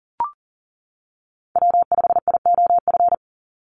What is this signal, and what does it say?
Morse code, heard as a keyed tone, about 40 wpm, 700 Hz. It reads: W5IOF
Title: USA Call Signs 40wpm